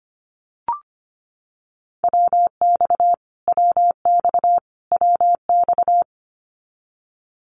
Morse code, heard as a keyed tone, about 25 wpm, 700 Hz. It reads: WX WX WX
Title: Abbreviations Repeated 3x - 25wpm